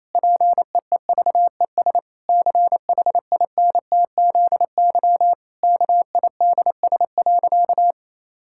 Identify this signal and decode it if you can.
Morse code, heard as a keyed tone, about 28 wpm, 700 Hz. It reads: PEEVES CHINTZY KIDS.
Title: Pangram Challenge v2 - 28wpm